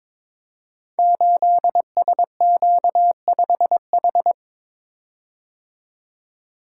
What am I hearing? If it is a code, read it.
Morse code, heard as a keyed tone, about 22 wpm, 700 Hz. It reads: 8SQ5H
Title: Code Groups - 5 Letter:Numbers - 22wpm